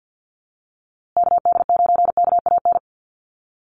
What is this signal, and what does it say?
Morse code, heard as a keyed tone, about 52 wpm, 700 Hz. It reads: KD9KAN
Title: USA Call Signs Repeated 3x 50wpm